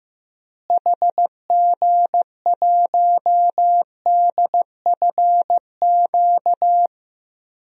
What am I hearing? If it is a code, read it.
Morse code, heard as a keyed tone, about 15 wpm, 700 Hz. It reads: HG1DFQ
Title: DXCC-100 Call Signs - 6 Characters - Repeated 3x - 15wpm